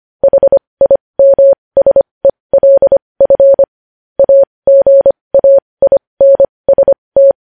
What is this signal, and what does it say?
Morse code, heard as a keyed tone, about 25 wpm, spaced as for 20 wpm, 550 Hz. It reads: HIMSELF AGAINST
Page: news headlines in morse code at 20 wpm